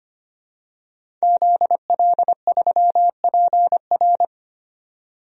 Morse code, heard as a keyed tone, about 25 wpm, 700 Hz. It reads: ZL3PR
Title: DXCC-100 Call Signs - 5 Characters - 25wpm